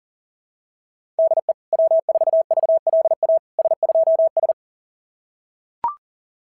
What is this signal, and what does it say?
Morse code, heard as a keyed tone, about 40 wpm, 660 Hz. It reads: DE W4VLA S2S
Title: SOTA QSO Practice 40wpm